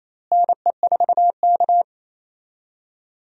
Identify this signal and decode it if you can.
Morse code, heard as a keyed tone, about 28 wpm, 700 Hz. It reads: NE4K
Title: USA Call Signs - 4 Characters - Repeated 3x - 28wpm